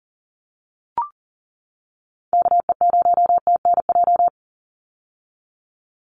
Morse code, heard as a keyed tone, about 40 wpm, 700 Hz. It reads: KE0TNJ
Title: USA Call Signs 40wpm